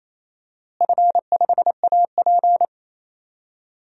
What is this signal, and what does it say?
Morse code, heard as a keyed tone, about 28 wpm, 700 Hz. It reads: F5AP
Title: DXCC-100 Call Signs - 4 Characters - Repeated 3x - 28wpm